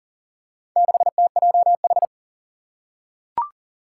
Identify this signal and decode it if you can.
Morse code, heard as a keyed tone, about 37 wpm, 700 Hz. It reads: 6TJH
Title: Four Challenging Letter-Number Combinations 40wpm